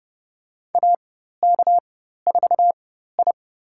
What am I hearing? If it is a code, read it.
Morse code, heard as a keyed tone, about 30 wpm, spaced as for 14 wpm, 700 Hz. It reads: AK4I
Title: USA Call Signs - 4 Characters - 4x Character Spacing - 30wpm